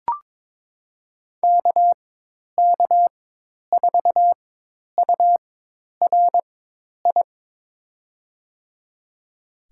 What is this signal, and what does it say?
Morse code, heard as a keyed tone, about 23 wpm, spaced as for 10 wpm, 700 Hz. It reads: KK4URI
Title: USA Call Signs - 4x Character Spacing - 22wpm